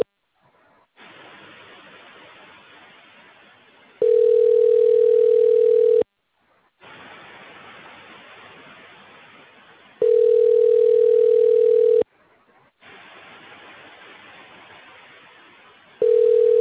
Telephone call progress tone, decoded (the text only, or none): ringback tone